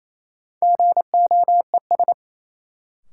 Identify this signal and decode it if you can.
Morse code, heard as a keyed tone, about 28 wpm, 700 Hz. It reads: GOES
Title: Sentences from Top 1000 Words - Review Part 5 - New Word Drill - 28wpm